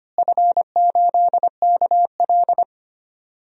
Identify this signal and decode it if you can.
Morse code, heard as a keyed tone, about 25 wpm, 700 Hz. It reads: F8KL
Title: DXCC-100 Call Signs - 4 Characters - Repeated 3x - 25wpm